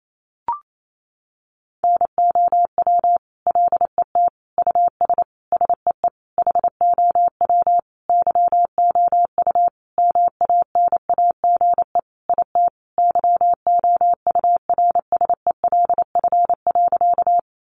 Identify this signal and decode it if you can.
Morse code, heard as a keyed tone, about 28 wpm, 700 Hz. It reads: NOW LET US SEE HOW YOU MANAGE IT YOURSELF.
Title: Sentences from Top 1400 Words 28wpm